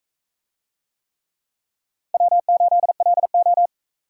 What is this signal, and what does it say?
Morse code, heard as a keyed tone, about 38 wpm, 700 Hz. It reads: W8LO